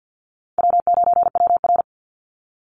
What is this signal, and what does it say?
Morse code, heard as a keyed tone, about 50 wpm, 700 Hz. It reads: W9WR
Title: Easy Call Signs - 50wpm